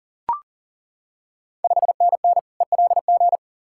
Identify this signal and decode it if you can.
Morse code, heard as a keyed tone, about 40 wpm, 690 Hz. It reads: 5NN ELG
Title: Ontario QSO Party Exchange 40wpm